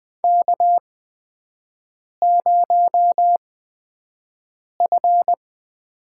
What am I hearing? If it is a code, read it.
Morse code, heard as a keyed tone, about 20 wpm, spaced as for 5 wpm, 700 Hz. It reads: K0F
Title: USA Call Signs - 4 Characters - 8x Character Spacing - 20wpm